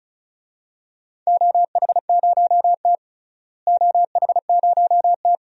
Morse code, heard as a keyed tone, about 34 wpm, 700 Hz. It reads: OH0T OH0T
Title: DXCC-100 Call Signs - 4 Characters - Repeated 2x - 35wpm